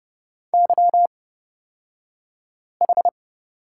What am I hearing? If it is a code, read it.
Morse code, heard as a keyed tone, about 30 wpm, spaced as for 5 wpm, 700 Hz. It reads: YH
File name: Lesson 36d - BT - Two Characters 30wpm